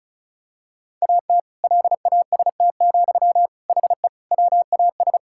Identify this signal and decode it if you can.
Morse code, heard as a keyed tone, about 34 wpm, 700 Hz. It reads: AT LAST, HE WAS
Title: Sentences from Top 1000 Words - Review Part 6 - 35wpm